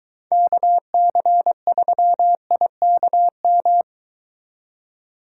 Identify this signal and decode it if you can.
Morse code, heard as a keyed tone, about 23 wpm, 700 Hz. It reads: KC3IKM